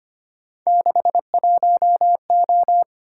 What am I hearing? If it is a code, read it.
Morse code, heard as a keyed tone, about 25 wpm, 700 Hz. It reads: 61O
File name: Three Letter-Number Combinations - 25wpm